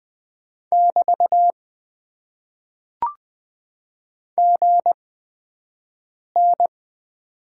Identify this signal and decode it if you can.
Morse code, heard as a keyed tone, about 20 wpm, spaced as for 5 wpm, 700 Hz. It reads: <BT> GN
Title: Top 90 QSO Elements - 8x Character Spacing - 20wpm